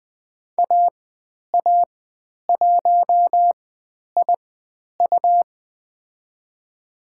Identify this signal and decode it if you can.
Morse code, heard as a keyed tone, about 20 wpm, spaced as for 10 wpm, 700 Hz. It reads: AA1IU
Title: 109 Lesson 18g - Letter m - Call Signs 20wpm